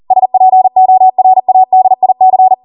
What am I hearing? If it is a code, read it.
Morse code, heard as a keyed tone, about 40 wpm, 750 Hz. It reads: SPORADIC